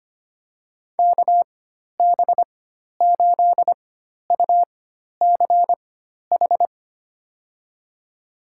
Morse code, heard as a keyed tone, about 25 wpm, spaced as for 12 wpm, 700 Hz. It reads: KB8UCH
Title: USA Call Signs - 6 Characters - 4x Character Spacing - 25wpm